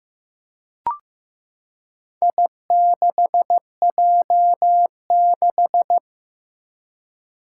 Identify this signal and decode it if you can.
Morse code, heard as a keyed tone, about 15 wpm, 700 Hz. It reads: I6J6
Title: Four Challenging Letter-Number Combinations 15wpm